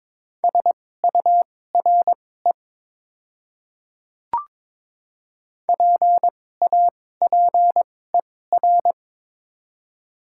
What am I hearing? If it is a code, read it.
Morse code, heard as a keyed tone, about 22 wpm, spaced as for 16 wpm, 700 Hz. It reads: SURE PAPER
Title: 2x Character Spacing - Top 300 Words - 22wpm